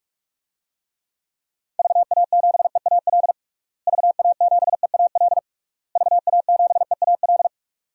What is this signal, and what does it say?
Morse code, heard as a keyed tone, about 43 wpm, 700 Hz. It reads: VA7EAL VA7EAL VA7EAL
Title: DXCC-100 Call Signs - 6 Characters - Repeated 3x - 45wpm